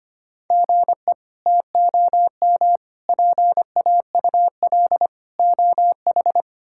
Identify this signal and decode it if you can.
Morse code, heard as a keyed tone, about 25 wpm, 700 Hz. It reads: GE TOM PAUL OH
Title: K1USN SST QSO Practice - 25wpm